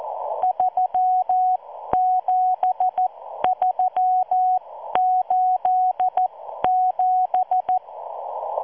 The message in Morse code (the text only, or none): 37387